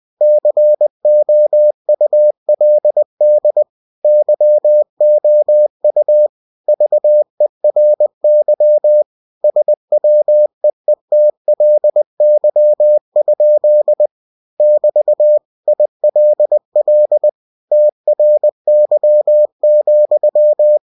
Morse code, heard as a keyed tone, about 20 wpm, 600 Hz. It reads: COULD YOU VERY SWEETLY? <BT> ILL TRY,